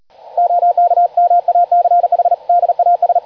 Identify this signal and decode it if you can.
Morse code, heard as a keyed tone, about 39 wpm, 700 Hz, with radio noise. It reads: OK MACH DAS
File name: digimodes / Digimodes mp3 / CW-Hispeed